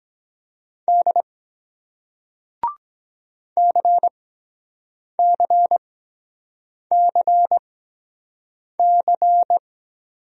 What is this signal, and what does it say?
Morse code, heard as a keyed tone, about 23 wpm, spaced as for 7 wpm, 700 Hz. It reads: D CCCC